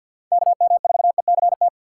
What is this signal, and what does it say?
Morse code, heard as a keyed tone, about 52 wpm, 700 Hz. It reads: KM4ECT
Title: USA Call Signs - 3x Character Spacing - 50wpm